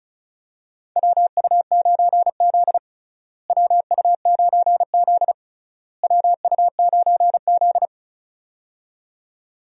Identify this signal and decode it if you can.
Morse code, heard as a keyed tone, about 34 wpm, 700 Hz. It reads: WU9Z WU9Z WU9Z
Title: USA Call Signs - 4 Characters - Repeated 3x - 35wpm